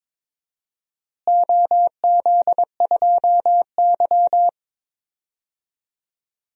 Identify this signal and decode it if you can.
Morse code, heard as a keyed tone, about 22 wpm, 700 Hz. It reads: OZ2Y
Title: DXCC-100 Call Signs - 4 Characters - Repeated 3x - 22wpm